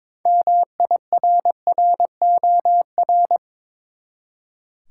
Morse code, heard as a keyed tone, about 22 wpm, 700 Hz. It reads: MIRROR
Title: Six Letter Words - 22wpm